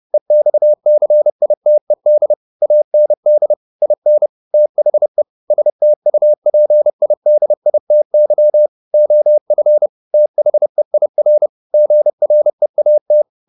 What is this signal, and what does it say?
Morse code, heard as a keyed tone, about 30 wpm, 600 Hz. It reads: EXCITED AND IN THE STUPIDITY OF THEIR GREAT